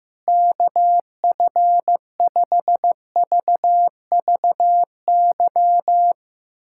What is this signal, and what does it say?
Morse code, heard as a keyed tone, about 15 wpm, 700 Hz. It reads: KF5VVY
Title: USA Callsign DE Callsign - 6 Characters - 15wpm